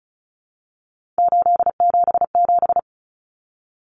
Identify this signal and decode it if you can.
Morse code, heard as a keyed tone, about 34 wpm, 700 Hz. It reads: 877
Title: Common Cut Numbers - 3-Digit 35wpm